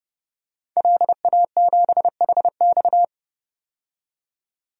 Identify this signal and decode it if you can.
Morse code, heard as a keyed tone, about 32 wpm, 700 Hz. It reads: LA7HX
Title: DXCC-100 Call Signs - 5 Characters - 30wpm